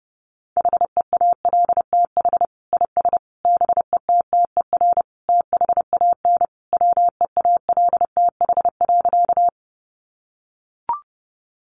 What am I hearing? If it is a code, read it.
Morse code, heard as a keyed tone, about 30 wpm, 700 Hz. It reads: HEALTH IS BETTER THAN WEALTH.
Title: Sentences from Top 1200 Words 30wpm